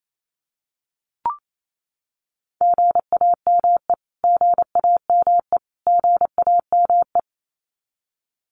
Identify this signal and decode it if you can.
Morse code, heard as a keyed tone, about 28 wpm, 700 Hz. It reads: GAME GAME GAME
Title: Top 500 Words Repeated 28wpm